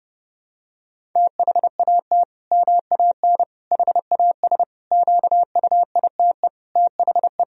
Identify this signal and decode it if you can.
Morse code, heard as a keyed tone, about 32 wpm, 700 Hz. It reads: THAT MAN HAS QUITE THE
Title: Sentences from Top 1000 Words - Review Part 1 - 30wpm